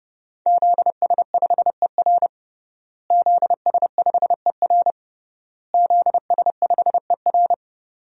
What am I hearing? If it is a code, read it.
Morse code, heard as a keyed tone, about 31 wpm, 700 Hz. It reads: ZS5ER ZS5ER ZS5ER
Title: DXCC-100 Call Signs - 5 Characters - Repeated 3x - 30wpm